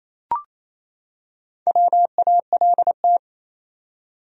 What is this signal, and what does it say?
Morse code, heard as a keyed tone, about 28 wpm, 700 Hz. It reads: WALT
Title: USA Common Names - 28wpm